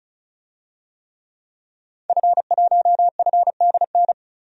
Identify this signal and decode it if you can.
Morse code, heard as a keyed tone, about 36 wpm, 700 Hz. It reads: F1FDN
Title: DXCC-100 Call Signs - 5 Characters - 35wpm